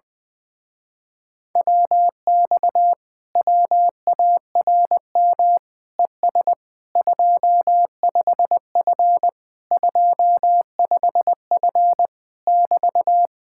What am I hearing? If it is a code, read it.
Morse code, heard as a keyed tone, about 20 wpm, 700 Hz. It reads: WX WARM ES 25F 25F <BT>
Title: Speed-Racing Common QSO Phrases - 20wpm